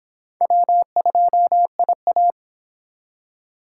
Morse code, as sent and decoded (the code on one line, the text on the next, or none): .-- ..--- .. .-
W2IA